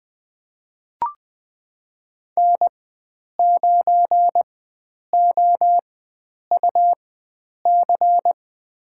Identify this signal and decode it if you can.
Morse code, heard as a keyed tone, about 20 wpm, spaced as for 9 wpm, 700 Hz. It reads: N9OUC
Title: USA Call Signs - 4x Character Spacing - 20wpm